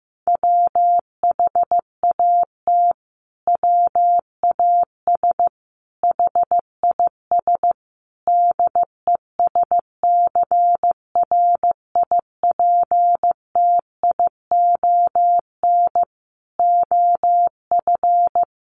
Morse code, heard as a keyed tone, about 15 wpm, 700 Hz. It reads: WHAT WAS HIS DESCRIPTION OF
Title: Sentences from Top 1300 Words 15wpm